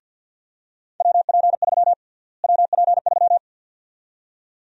Morse code, heard as a keyed tone, about 50 wpm, 700 Hz. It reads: WP3 WP3
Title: USA Call Sign Prefixes - Repeated 2x - 50wpm